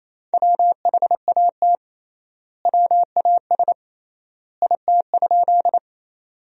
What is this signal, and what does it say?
Morse code, heard as a keyed tone, about 28 wpm, 700 Hz. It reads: WHAT WAS IT?
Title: 3x Word-Spacing - Sentences from Top 100 Words - 28wpm